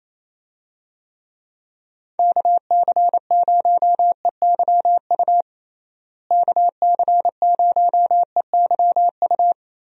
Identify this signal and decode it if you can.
Morse code, heard as a keyed tone, about 28 wpm, 700 Hz. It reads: KC0EYU KC0EYU
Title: USA Call Signs - 6 Characters - Repeated 3x - 28wpm